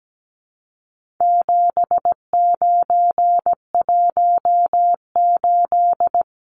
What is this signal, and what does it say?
Morse code, heard as a keyed tone, about 17 wpm, 700 Hz. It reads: 7918
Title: Numbers - 4-Digit 17wpm